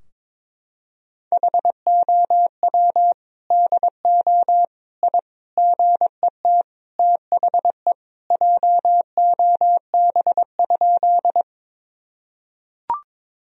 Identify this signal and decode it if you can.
Morse code, heard as a keyed tone, about 22 wpm, 700 Hz. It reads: HOW DO I GET THE JOB?
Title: Sentences from Top 1000 Words - Review Part 3 - 22wpm